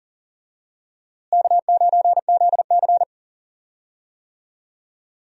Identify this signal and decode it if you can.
Morse code, heard as a keyed tone, about 40 wpm, 700 Hz. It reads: K9ZC
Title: USA Call Signs - 4 Characters - Repeated 3x - 40wpm